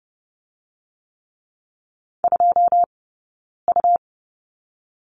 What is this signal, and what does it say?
Morse code, heard as a keyed tone, about 30 wpm, 700 Hz. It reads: 2 U
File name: Lesson 17d - Letter u - Two Characters 30wpm